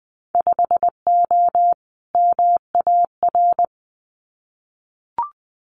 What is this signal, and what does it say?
Morse code, heard as a keyed tone, about 20 wpm, 700 Hz. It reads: 5O MAR
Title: Winter Field Day Exchanges Encore 20wpm